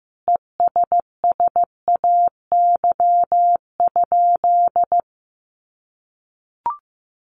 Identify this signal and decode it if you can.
Morse code, heard as a keyed tone, about 15 wpm, 700 Hz. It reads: ESSAY?
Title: Sentences from Top 1500 Words 15wpm